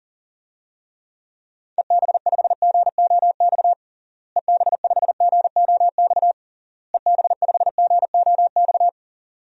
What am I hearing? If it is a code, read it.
Morse code, heard as a keyed tone, about 37 wpm, 700 Hz. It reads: EB5GOX EB5GOX EB5GOX
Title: DXCC-100 Call Signs - 6 Characters - Repeated 3x - 40wpm